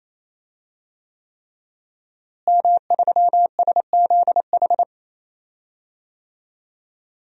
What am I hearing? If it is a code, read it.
Morse code, heard as a keyed tone, about 27 wpm, 700 Hz. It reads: M3SZH
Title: DXCC-100 Call Signs - 5 Characters - Repeated 3x - 28wpm